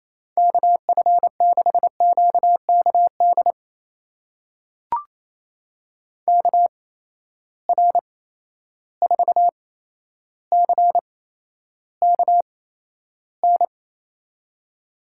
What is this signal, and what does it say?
Morse code, heard as a keyed tone, about 28 wpm, spaced as for 12 wpm, 700 Hz. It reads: KF6QKD KR4CKN
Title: USA Call Signs - 6 Characters - 8x Character Spacing - 28wpm